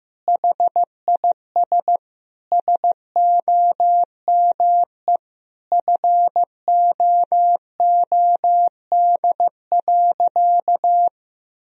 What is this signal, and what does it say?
Morse code, heard as a keyed tone, about 16 wpm, 700 Hz. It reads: HIS SOME FOOD.
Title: Sentences from Top 1000 Words - Review Part 1 - 15wpm